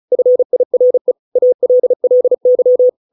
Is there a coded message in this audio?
Yes: Morse code from 0.1 to 2.9 s, heard as a keyed tone, about 34 wpm, 500 Hz. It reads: FIRE ALLY